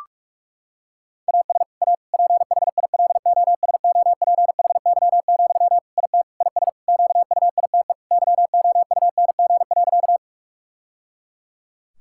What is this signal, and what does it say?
Morse code, heard as a keyed tone, about 46 wpm, 700 Hz. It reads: AS A PHILOSOPHY, IT IS QUITE YOUNG.
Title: Sentences from Top 1000 Words - Review Part 6 - 45wpm